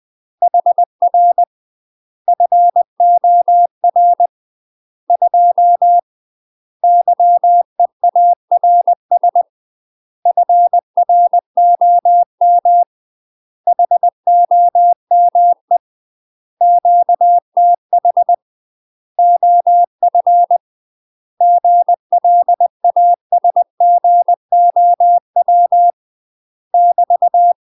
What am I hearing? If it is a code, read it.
Morse code, heard as a keyed tone, about 20 wpm, 700 Hz. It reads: HR FOR 2 YEARS FROM HOME QTH OF GLASGOW <BT>
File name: UK MASTER QSOs TRAINING / 20WPM / UK MASTER QSOs 0049